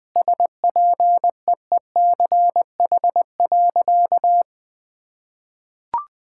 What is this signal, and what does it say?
Morse code, heard as a keyed tone, about 21 wpm, 700 Hz. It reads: SPEECH.
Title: Sentences with 5-Word Phrases - 20wpm